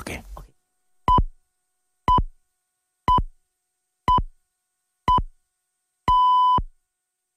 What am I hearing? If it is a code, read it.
Radio time signal (time pips), with speech. It is heard from 1.1 to 6.6 s.